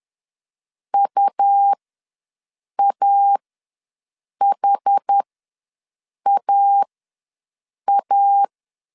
Morse code, heard as a keyed tone, about 10 wpm, spaced as for 6 wpm, 800 Hz. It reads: UAHAA
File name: LEZIONI / 5 LBVC / 5 LB ws 10wpm 3616